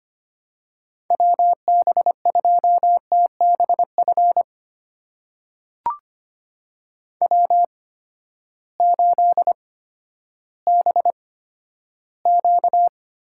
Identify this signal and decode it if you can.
Morse code, heard as a keyed tone, about 25 wpm, spaced as for 13 wpm, 700 Hz. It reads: WB2TBF W8BQ
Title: USA Call Signs - 8x Character Spacing - 25wpm